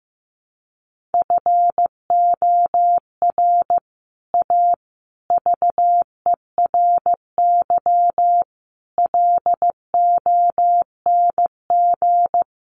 Morse code, heard as a keyed tone, about 15 wpm, 700 Hz. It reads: FOR A VERY LONG
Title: Sets of 5 Words 15wpm